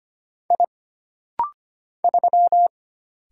Morse code, heard as a keyed tone, about 25 wpm, 700 Hz. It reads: I 3
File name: Single Character (Letter:Number:Punctuation:Prosign) - Rapid-Fire - 25wpm